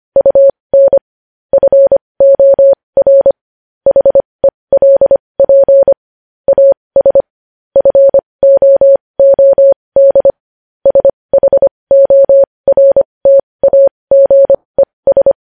Morse code, heard as a keyed tone, about 25 wpm, spaced as for 20 wpm, 550 Hz. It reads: UN FOR HELP AS FOOD SHORTAGES
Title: News Headlines at 20